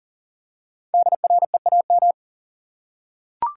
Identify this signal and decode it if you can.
Morse code, heard as a keyed tone, about 38 wpm, 700 Hz. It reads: DREAM
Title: Five Letter Words - 40wpm